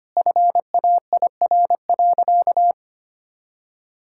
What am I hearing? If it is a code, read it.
Morse code, heard as a keyed tone, about 25 wpm, 700 Hz. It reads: FAIR.
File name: practice / 25wpm / Sentences with 3-Word Phrases - 25wpm